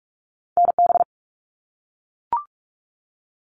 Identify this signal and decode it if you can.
Morse code, heard as a keyed tone, about 44 wpm, 700 Hz. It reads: NB
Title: Canadian Provinces and Territories - Repeated 3x - 45wpm